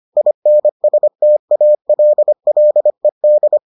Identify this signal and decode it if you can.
Morse code, heard as a keyed tone, about 25 wpm, 600 Hz. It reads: INSTALLED